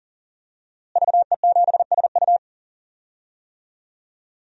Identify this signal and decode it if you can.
Morse code, heard as a keyed tone, about 40 wpm, 700 Hz. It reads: VE7SU
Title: DXCC-100 Call Signs - 5 Characters - Repeated 2x - 40wpm